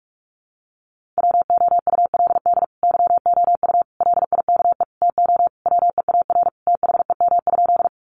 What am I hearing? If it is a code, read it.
Morse code, heard as a keyed tone, about 44 wpm, 700 Hz. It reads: WOULD YOU LIKE TO WEAR THEM?
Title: Sentences from Top 1500 Words 45wpm